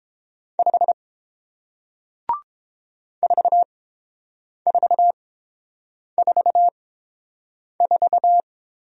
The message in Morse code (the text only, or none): 5 4444